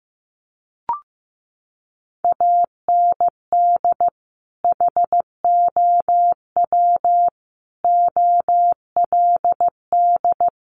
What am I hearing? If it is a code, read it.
Morse code, heard as a keyed tone, about 15 wpm, 700 Hz. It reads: AND HOW OLD
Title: Sentences from Top 200 Words 15wpm